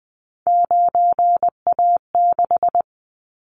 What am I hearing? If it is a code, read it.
Morse code, heard as a keyed tone, about 20 wpm, 700 Hz. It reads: 9A6
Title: Callsign Prefixes for Top 50 DXCC - 20wpm